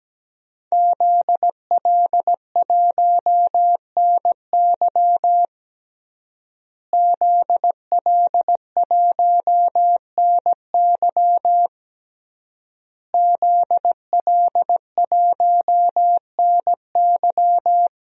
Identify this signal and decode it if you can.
Morse code, heard as a keyed tone, about 17 wpm, 700 Hz. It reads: ZL1NY ZL1NY ZL1NY
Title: DXCC-100 Call Signs - 5 Characters - Repeated 3x - 17wpm